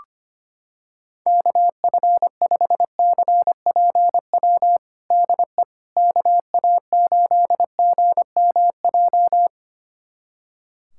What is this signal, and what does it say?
Morse code, heard as a keyed tone, about 25 wpm, 700 Hz. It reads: KF5CPW DE KA8GMJ
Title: USA Callsign DE Callsign - 6 Characters - 25wpm